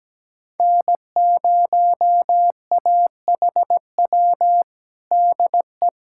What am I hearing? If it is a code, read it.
Morse code, heard as a keyed tone, about 17 wpm, 700 Hz. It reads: N0AHW DE
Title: USA Callsign DE Callsign - 5 Characters - 17wpm